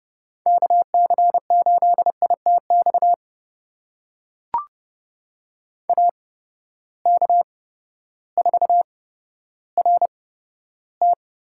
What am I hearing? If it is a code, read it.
Morse code, heard as a keyed tone, about 29 wpm, spaced as for 14 wpm, 700 Hz. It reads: KC8ITX AK4RT
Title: USA Call Signs - 8x Character Spacing - 30wpm